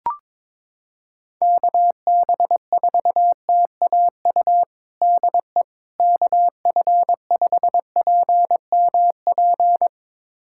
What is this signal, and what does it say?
Morse code, heard as a keyed tone, about 22 wpm, 700 Hz. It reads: KB4TAU DE KF5PMP
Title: USA Callsign DE Callsign - 6 Characters - 22wpm